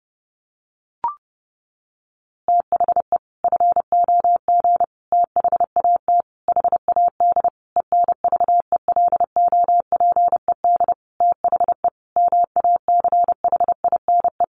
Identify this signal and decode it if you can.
Morse code, heard as a keyed tone, about 30 wpm, 700 Hz. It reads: THE FOG THAT HAD ENVELOPED THE MACHINE
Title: A Sound Of Thunder 30wpm